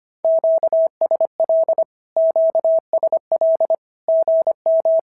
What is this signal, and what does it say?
Morse code, heard as a keyed tone, about 25 wpm, 650 Hz. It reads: QSL QSL GM
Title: SOTA QSO Practice 25wpm